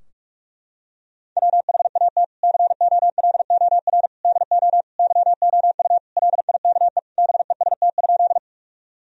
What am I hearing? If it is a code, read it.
Morse code, heard as a keyed tone, about 46 wpm, 700 Hz. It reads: WHAT COLOR DO YOU LIKE BEST?
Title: Sentences from Top 1000 Words - Review Part 2 - 45wpm